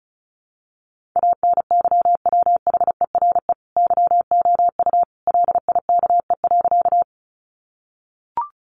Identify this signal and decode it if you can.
Morse code, heard as a keyed tone, about 34 wpm, 700 Hz. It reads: ANYWHERE YOU LIKE.
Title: Sentences from Top 1400 Words 35wpm